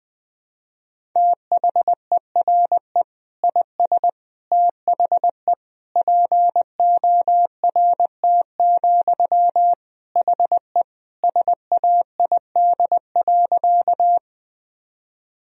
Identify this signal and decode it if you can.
Morse code, heard as a keyed tone, about 20 wpm, 700 Hz. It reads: THERE IS THE PORT, HE SAID.
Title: Sentences from Top 1000 Words - Review Part 1 - 20wpm